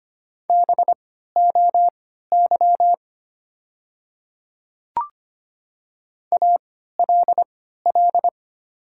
Morse code, heard as a keyed tone, about 25 wpm, spaced as for 14 wpm, 700 Hz. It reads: BOY ALL
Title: 3x Character Spacing - Top 300 Words - 25wpm